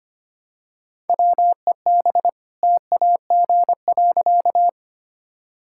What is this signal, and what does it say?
Morse code, heard as a keyed tone, about 25 wpm, 700 Hz. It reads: WEB TAG.
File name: practice / 25wpm / All-Age Appropriate Jokes - 25wpm